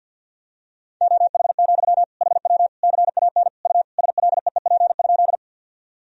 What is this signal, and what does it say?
Morse code, heard as a keyed tone, about 52 wpm, 700 Hz. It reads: OH, HW CAN U SLEEP?
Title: Sentences from Top 500 Words and Abbreviations 50wpm